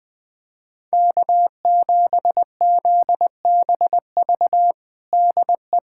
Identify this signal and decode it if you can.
Morse code, heard as a keyed tone, about 21 wpm, 700 Hz. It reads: K7ZBV DE
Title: USA Callsign DE Callsign - 5 Characters - 20wpm